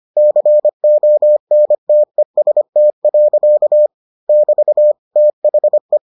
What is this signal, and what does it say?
Morse code, heard as a keyed tone, about 25 wpm, 600 Hz. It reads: CONTEST. <BT> THE